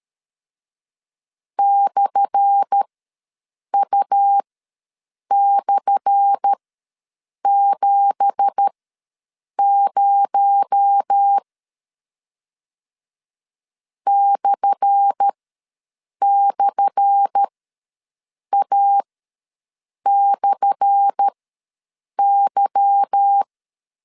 Morse code, heard as a keyed tone, about 13 wpm, spaced as for 7 wpm, 800 Hz. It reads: /U/70 //A/Y
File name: LEZIONI / 10 SIGNS / 10A SLASH ws 12wpm 3443